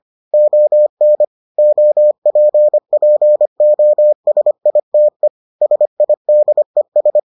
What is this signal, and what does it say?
Morse code, heard as a keyed tone, about 25 wpm, 600 Hz. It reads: ON OPPOSITE SIDES